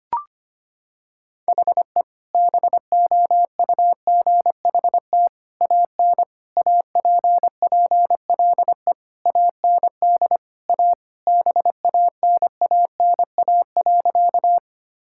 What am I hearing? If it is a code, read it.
Morse code, heard as a keyed tone, about 25 wpm, 700 Hz. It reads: HE BOUGHT AN APPLE AND A BANANA.
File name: practice / 25wpm / Sentences with 2-Word Phrases - 25wpm